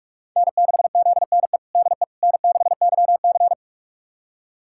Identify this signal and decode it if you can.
Morse code, heard as a keyed tone, about 46 wpm, 700 Hz. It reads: N6ZNE DE N6YC
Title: USA Callsign DE Callsign 45wpm